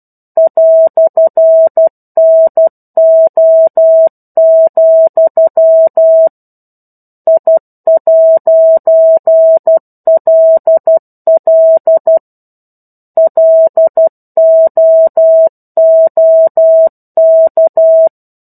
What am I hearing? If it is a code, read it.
Morse code, heard as a keyed tone, about 12 wpm, 650 Hz. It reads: "NO, I'LL LOOK
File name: 01-Chapter-01 12 WPM